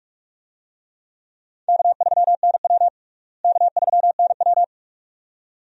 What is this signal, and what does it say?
Morse code, heard as a keyed tone, about 43 wpm, 700 Hz. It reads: K3NW K3NW
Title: USA Call Signs - 4 Characters - Repeated 2x - 45wpm